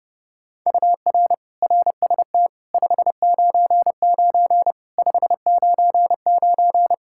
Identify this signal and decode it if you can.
Morse code, heard as a keyed tone, about 30 wpm, 700 Hz. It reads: UR RST 599 599
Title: Practice QSO - Exchange 1 - Beginning - 30wpm